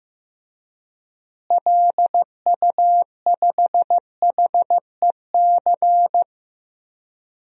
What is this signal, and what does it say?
Morse code, heard as a keyed tone, about 15 wpm, 700 Hz. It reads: LU5HEC
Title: DXCC-100 Call Signs - 6 Characters - Repeated 2x - 15wpm